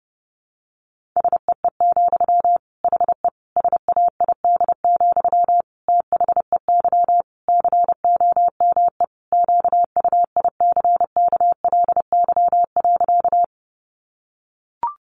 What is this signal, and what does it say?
Morse code, heard as a keyed tone, about 28 wpm, 700 Hz. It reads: SEE, HE SAID, THEY COME QUICKLY.
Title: Sentences from Top 700 Words 30wpm